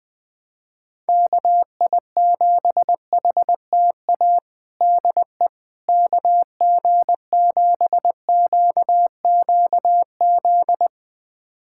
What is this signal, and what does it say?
Morse code, heard as a keyed tone, about 20 wpm, 700 Hz. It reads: KI7HTA DE KG7QQZ
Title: USA Callsign DE Callsign - 6 Characters - 20wpm